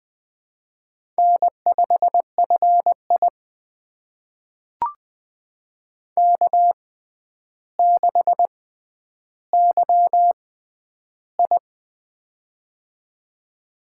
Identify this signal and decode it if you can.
Morse code, heard as a keyed tone, about 20 wpm, spaced as for 10 wpm, 700 Hz. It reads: N5FI K6YI
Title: USA Call Signs - 4 Characters - 6x Character Spacing - 20wpm